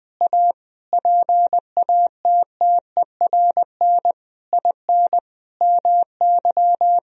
Morse code, heard as a keyed tone, about 20 wpm, 700 Hz. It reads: A PATTERN IN MY
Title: Sentences from Top 1000 Words - Review Part 2 - 20wpm